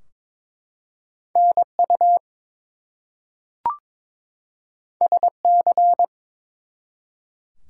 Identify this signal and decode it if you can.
Morse code, heard as a keyed tone, about 22 wpm, 700 Hz. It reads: NU SC
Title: US States and Canadian Provinces 22wpm